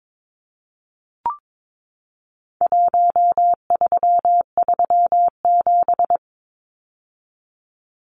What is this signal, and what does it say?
Morse code, heard as a keyed tone, about 22 wpm, 700 Hz. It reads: 1337
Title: Numbers - 4-Digit 22wpm